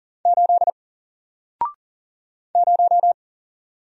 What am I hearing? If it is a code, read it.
Morse code, heard as a keyed tone, about 40 wpm, 700 Hz. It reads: 8 0
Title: Single Character (Letter:Number:Punctuation:Prosign) 40wpm